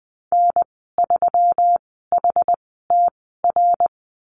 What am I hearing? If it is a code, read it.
Morse code, heard as a keyed tone, about 20 wpm, spaced as for 15 wpm, 700 Hz. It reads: N3HTR
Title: 128 Lesson 21g - Number 6 - Call Signs 20wpm